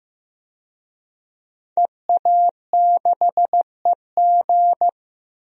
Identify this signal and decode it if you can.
Morse code, heard as a keyed tone, about 15 wpm, 700 Hz. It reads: EA6EG
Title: DXCC-100 Call Signs - 5 Characters - 15wpm